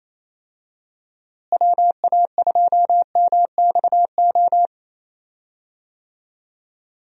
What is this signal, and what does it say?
Morse code, heard as a keyed tone, about 28 wpm, 700 Hz. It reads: WA2MXO
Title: USA Call Signs - 6 Characters - Repeated 2x - 28wpm